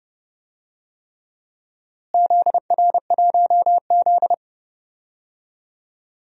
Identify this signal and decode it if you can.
Morse code, heard as a keyed tone, about 30 wpm, 700 Hz. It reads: ZR1Z
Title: DXCC-100 Call Signs - 4 Characters - Repeated 3x - 30wpm